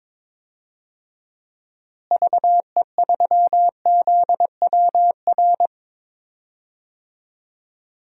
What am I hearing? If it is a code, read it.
Morse code, heard as a keyed tone, about 22 wpm, 700 Hz. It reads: VE3ZWR
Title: DXCC-100 Call Signs - 6 Characters - Repeated 3x - 22wpm